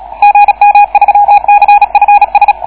Morse code, heard as a keyed tone, about 34 wpm, 810 Hz. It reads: GM3CFS